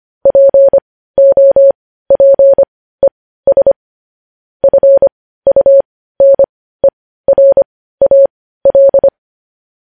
Morse code, heard as a keyed tone, about 25 wpm, spaced as for 15 wpm, 550 Hz. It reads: POPES FUNERAL
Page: news-headlines-in-morse-code-at-15-wpm